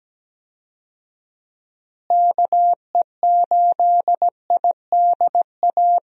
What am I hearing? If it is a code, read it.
Morse code, heard as a keyed tone, about 17 wpm, 700 Hz. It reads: KE8IDA
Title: USA Call Signs - 6 Characters - Repeated 3x - 17wpm